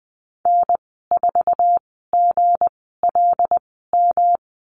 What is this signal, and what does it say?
Morse code, heard as a keyed tone, about 20 wpm, spaced as for 14 wpm, 700 Hz. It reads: N4GLM
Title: 161 Lesson 26g - Letter g - Call Signs 20wpm